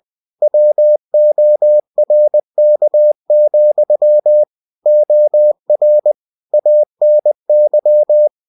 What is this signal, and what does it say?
Morse code, heard as a keyed tone, about 20 wpm, 600 Hz. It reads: WORK, OR ANY